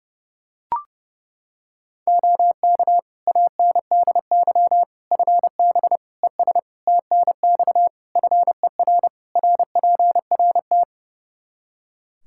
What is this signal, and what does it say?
Morse code, heard as a keyed tone, about 30 wpm, 700 Hz. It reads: OK ANDY FB ES TNX FER RPRT
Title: Practice QSO - Exchange 2 - Info - 30wpm